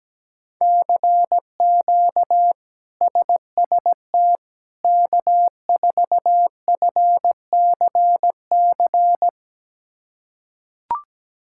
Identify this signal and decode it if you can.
Morse code, heard as a keyed tone, about 17 wpm, 700 Hz. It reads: CQ SST K4FCC
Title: K1USN SST QSO Practice - 17wpm